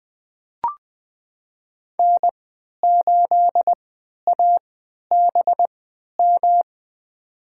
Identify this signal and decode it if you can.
Morse code, heard as a keyed tone, about 20 wpm, spaced as for 11 wpm, 700 Hz. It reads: N8ABM
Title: USA Call Signs - 5 Characters - 3x Character Spacing - 20wpm